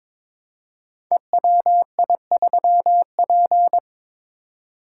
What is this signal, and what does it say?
Morse code, heard as a keyed tone, about 22 wpm, 700 Hz. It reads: EWI3P
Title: Code Groups - 5 Letter:Numbers - 22wpm